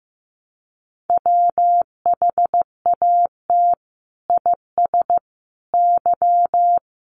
Morse code, heard as a keyed tone, about 14 wpm, 700 Hz. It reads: WHAT IS Y
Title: Sentences from Top 1000 Words 15wpm